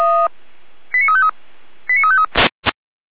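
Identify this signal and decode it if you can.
Five-tone selective call (ZVEI), heard as quick runs of five tones, tones of about 80 ms. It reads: 89353 89353
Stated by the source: ZVEI five-tone selective call received on an SDR.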